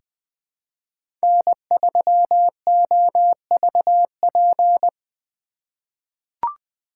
Morse code, heard as a keyed tone, about 20 wpm, 700 Hz. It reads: N3OVP
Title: USA Call Signs - 5 Characters - 3x Character Spacing - 20wpm